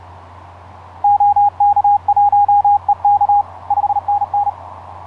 Morse code, heard as a keyed tone, about 35 wpm, 820 Hz, with radio noise. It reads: OK1EK 5NN
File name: 5r8x iz5ebl etc 12m cw 07nov2010 1003